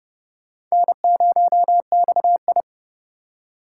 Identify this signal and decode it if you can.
Morse code, heard as a keyed tone, about 35 wpm, 700 Hz. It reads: N0XI
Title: Speed Racing - USA 1x2 Call Signs 30wpm